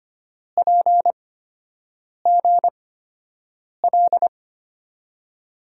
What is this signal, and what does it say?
Morse code, heard as a keyed tone, about 25 wpm, spaced as for 7 wpm, 700 Hz. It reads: PGL
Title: USA Call Signs - 6 Characters - 8x Character Spacing - 25wpm